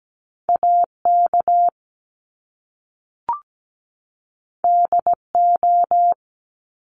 Morse code, heard as a keyed tone, about 17 wpm, 700 Hz. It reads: AK DO
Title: Top Two Letter Combinations in Words 17wpm